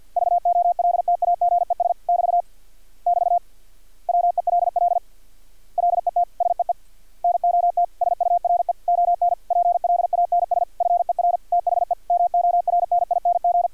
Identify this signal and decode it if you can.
Morse code, heard as a keyed tone, about 50 wpm, 700 Hz. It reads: VOLTAGES <BT> <BT> WELL LET SEE NOT SURE ON PLANS FER THE MORNING
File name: archived / CRSnet